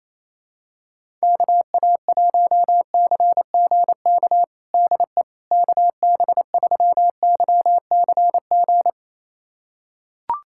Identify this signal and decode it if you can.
Morse code, heard as a keyed tone, about 28 wpm, 700 Hz. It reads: KA1CGK DE KB3YCG
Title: USA Callsign DE Callsign - 6 Characters - 28wpm